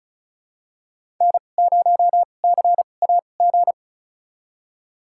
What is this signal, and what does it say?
Morse code, heard as a keyed tone, about 36 wpm, spaced as for 25 wpm, 700 Hz. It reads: N0CAG
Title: USA Call Signs - 5 Characters - 2x Character Spacing - 35wpm